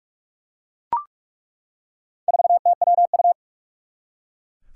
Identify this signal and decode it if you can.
Morse code, heard as a keyed tone, about 46 wpm, 700 Hz. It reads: 4TWU
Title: Four Challenging Letter-Number Combinations 45wpm